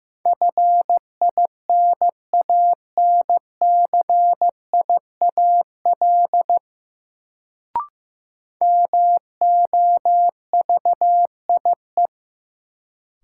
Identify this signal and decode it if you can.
Morse code, heard as a keyed tone, about 16 wpm, 700 Hz. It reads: FINANCIAL MOVIE
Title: Sentences from Top 1000 Words - Review Part 5 - New Word Drill - 15wpm